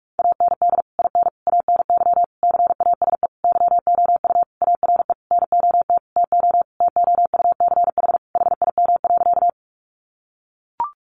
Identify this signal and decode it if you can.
Morse code, heard as a keyed tone, about 44 wpm, 700 Hz. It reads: AND IN ANY CASE YOU ARE NOT TO TOUCH HIM.
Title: Sentences from Top 1200 Words 45wpm